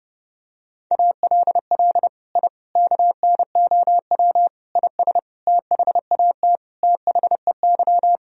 Morse code, heard as a keyed tone, about 30 wpm, 700 Hz. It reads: ALL I KNOW IS THAT THEY
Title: Sentences from Top 1000 Words - Review Part 1 - 30wpm